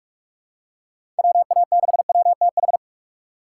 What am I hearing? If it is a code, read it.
Morse code, heard as a keyed tone, about 44 wpm, 700 Hz. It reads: WA6WTH